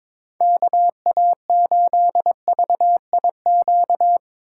Morse code, heard as a keyed tone, about 22 wpm, 700 Hz. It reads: KA8VIQ
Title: USA Call Signs - 6 Characters - 2x Character Spacing - 22wpm